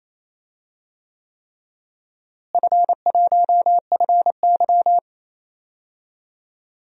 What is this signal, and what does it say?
Morse code, heard as a keyed tone, about 27 wpm, 700 Hz. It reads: F1FY
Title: DXCC-100 Call Signs - 4 Characters - Repeated 3x - 28wpm